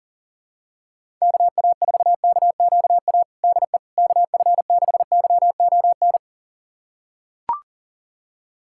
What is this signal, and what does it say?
Morse code, heard as a keyed tone, about 40 wpm, 700 Hz. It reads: KA4KQA DE KF6YON
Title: USA Callsign DE Callsign 40wpm